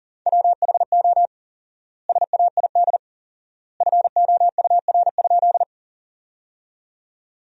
Morse code, heard as a keyed tone, about 37 wpm, 700 Hz. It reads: WHO SAID FOUR?
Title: 4x Word-Spacing - Sentences from Top 300 Words - 40wpm